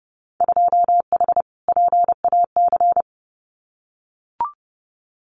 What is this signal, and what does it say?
Morse code, heard as a keyed tone, about 30 wpm, 700 Hz. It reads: 2H PAC
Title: Winter Field Day Exchanges Encore 30wpm